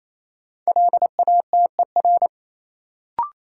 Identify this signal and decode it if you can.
Morse code, heard as a keyed tone, about 28 wpm, 700 Hz. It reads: LATER
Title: Sentences from Top 700 Words - New Word Drill - 28wpm